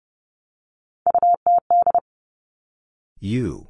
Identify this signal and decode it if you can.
Morse code, heard as a keyed tone, about 32 wpm, 700 Hz. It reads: UTD
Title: Three Challenging Letter-Number Combinations 30wpm